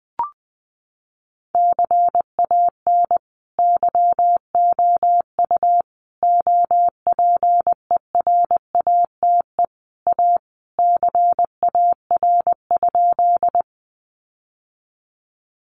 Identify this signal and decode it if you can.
Morse code, heard as a keyed tone, about 20 wpm, 700 Hz. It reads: CAN YOU OPERATE A CAR?
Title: Sentences from Top 1500 Words 20wpm